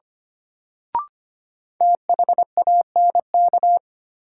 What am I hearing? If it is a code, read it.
Morse code, heard as a keyed tone, about 25 wpm, 700 Hz. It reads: THANK
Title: Sets of 2 Words - New Word Drill - 25wpm